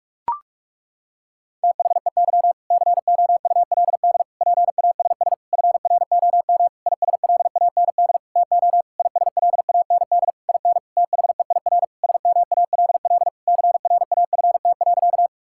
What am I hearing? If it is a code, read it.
Morse code, heard as a keyed tone, about 46 wpm, 700 Hz. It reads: THEY COULD PASS FROM ISLAND TO ISLAND IN THEIR SMALL CRAFT.
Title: Sentences from Top 1000 Words - Review Part 6 - 45wpm